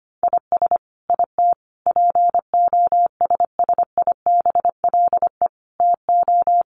Morse code, heard as a keyed tone, about 25 wpm, 700 Hz. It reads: IS IT POSSIBLE TO
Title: Sentences from Top 1200 Words 25wpm